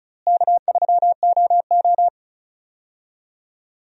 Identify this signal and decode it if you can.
Morse code, heard as a keyed tone, about 36 wpm, 700 Hz. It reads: K3OO
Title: USA Call Signs - 4 Characters - Repeated 3x - 35wpm